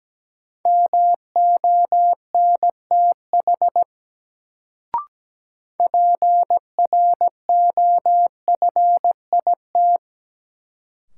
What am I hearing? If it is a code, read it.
Morse code, heard as a keyed tone, about 17 wpm, 700 Hz. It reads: MONTH PROFIT
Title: Sentences from Top 800 Words - New Word Drill - 17wpm